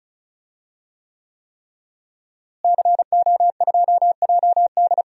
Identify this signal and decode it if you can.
Morse code, heard as a keyed tone, about 34 wpm, 700 Hz. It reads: CO2JD
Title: DXCC-100 Call Signs - 5 Characters - Repeated 3x - 35wpm